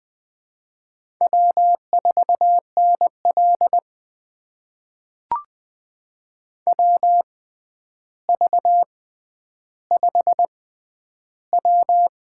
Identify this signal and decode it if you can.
Morse code, heard as a keyed tone, about 20 wpm, spaced as for 10 wpm, 700 Hz. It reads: W4NL WV5W
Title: USA Call Signs - 4 Characters - 6x Character Spacing - 20wpm